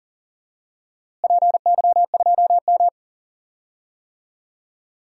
Morse code, heard as a keyed tone, about 40 wpm, 700 Hz. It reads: PY2M
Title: DXCC-100 Call Signs - 4 Characters - Repeated 3x - 40wpm